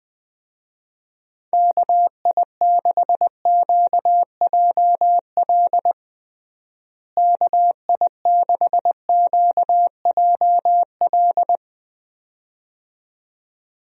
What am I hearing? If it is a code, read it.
Morse code, heard as a keyed tone, about 20 wpm, 700 Hz. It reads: KI6QJL KI6QJL
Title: USA Call Signs - 6 Characters - Repeated 2x - 20wpm